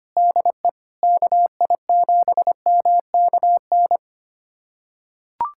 Morse code, heard as a keyed tone, about 25 wpm, 700 Hz. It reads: DE KI7MKN
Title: USA Callsign DE Callsign - 6 Characters - 25wpm